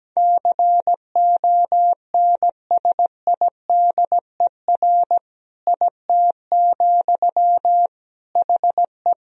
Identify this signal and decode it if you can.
Morse code, heard as a keyed tone, about 17 wpm, 700 Hz. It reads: CONSIDER IT, HE
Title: Sentences from Top 1000 Words - Review Part 4 - 17wpm